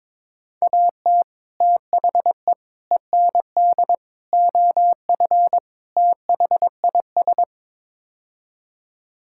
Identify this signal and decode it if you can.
Morse code, heard as a keyed tone, about 22 wpm, 700 Hz. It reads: AT THE END OF THIS
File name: practice / 22wpm / Build A Phrase - 5-Word Phrases - 22wpm